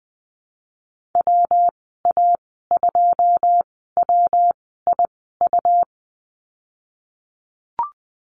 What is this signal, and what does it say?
Morse code, heard as a keyed tone, about 20 wpm, spaced as for 14 wpm, 700 Hz. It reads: WA2WIU
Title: 140 Lesson 23g - Letter f - Call Signs 20wpm